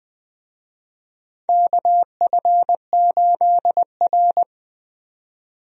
Morse code, heard as a keyed tone, about 21 wpm, 700 Hz. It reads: KF8R
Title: USA Call Signs - 4 Characters - 20wpm